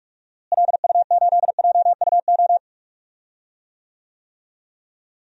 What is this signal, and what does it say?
Morse code, heard as a keyed tone, about 45 wpm, 700 Hz. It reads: LU8JUO
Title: DXCC-100 Call Signs - 6 Characters - Repeated 3x - 45wpm